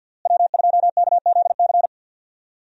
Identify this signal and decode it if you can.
Morse code, heard as a keyed tone, about 46 wpm, 690 Hz. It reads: W2KZC
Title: USA Call Signs - 5 Characters - Repeated 3x - 50wpm